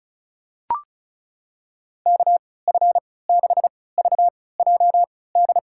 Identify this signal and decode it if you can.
Morse code, heard as a keyed tone, about 36 wpm, spaced as for 20 wpm, 700 Hz. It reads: KF6VJD
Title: USA Call Signs - 6 Characters - 3x Character Spacing - 35wpm